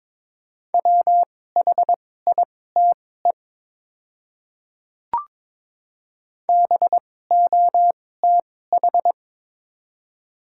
Morse code, heard as a keyed tone, about 22 wpm, spaced as for 16 wpm, 700 Hz. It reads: WHITE BOTH
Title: 2x Character Spacing - Top 500 Words - 22wpm